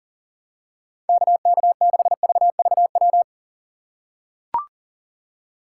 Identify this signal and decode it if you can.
Morse code, heard as a keyed tone, about 37 wpm, 700 Hz. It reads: KK6VVW
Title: USA Call Signs - 6 Characters - 2x Character Spacing - 40wpm